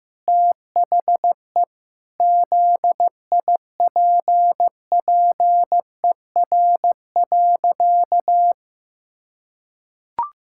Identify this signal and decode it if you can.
Morse code, heard as a keyed tone, about 15 wpm, 700 Hz. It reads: THE ZIPPER.
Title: Pangram Challenge v2 - 15wpm